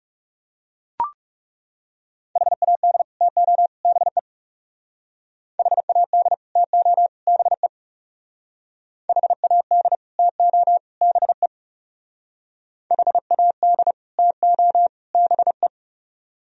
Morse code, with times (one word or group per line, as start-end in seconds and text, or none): 2.4-3.0 HAD
3.2-3.7 TO
3.9-4.2 BE
5.6-6.3 HAD
6.6-7.1 TO
7.3-7.7 BE
9.1-10.0 HAD
10.2-10.8 TO
11.0-11.5 BE
12.9-13.9 HAD
14.2-14.9 TO
15.1-15.7 BE